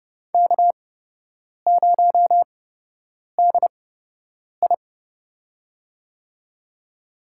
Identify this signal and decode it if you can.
Morse code, heard as a keyed tone, about 30 wpm, spaced as for 8 wpm, 700 Hz. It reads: K0DI
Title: USA Call Signs - 4 Characters - 8x Character Spacing - 30wpm